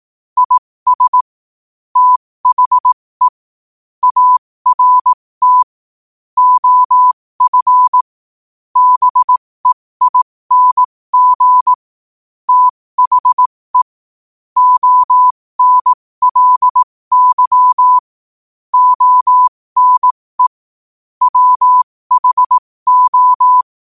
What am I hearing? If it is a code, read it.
Morse code, heard as a keyed tone, about 18 wpm, spaced as for 15 wpm, 1000 Hz. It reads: IS THE ART OF BEING THE ONLY ONE WHO
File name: qotdfiles / 15wpm